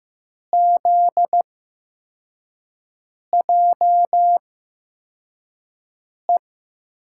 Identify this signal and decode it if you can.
Morse code, heard as a keyed tone, about 16 wpm, spaced as for 4 wpm, 700 Hz. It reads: ZJE
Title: USA Call Signs - 8x Character Spacing - 15wpm